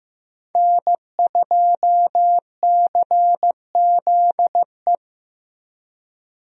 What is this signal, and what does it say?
Morse code, heard as a keyed tone, about 15 wpm, 700 Hz. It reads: N2CZE
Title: USA Call Signs - 5 Characters - Repeated 2x - 15wpm